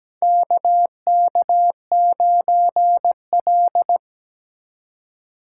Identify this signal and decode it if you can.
Morse code, heard as a keyed tone, about 17 wpm, 700 Hz. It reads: KK9L
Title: USA Call Signs - 4 Characters - 4x Character Spacing - 17wpm